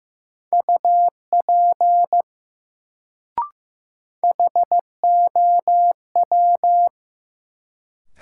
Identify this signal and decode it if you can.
Morse code, heard as a keyed tone, about 15 wpm, 700 Hz. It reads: UP HOW
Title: Sentences from Top 100 Words - New Word Drill - 15wpm